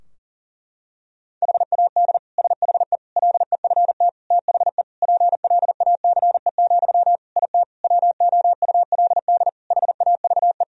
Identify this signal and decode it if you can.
Morse code, heard as a keyed tone, about 40 wpm, 700 Hz. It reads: HAD SHE LEFT THE PLACE, IT WOULD HAVE
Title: Sentences from Top 1000 Words - Review Part 3 - 40wpm